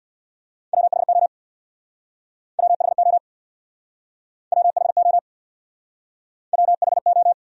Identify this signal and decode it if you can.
Morse code, heard as a keyed tone, about 67 wpm, 700 Hz. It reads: WHO WHO WHO WHO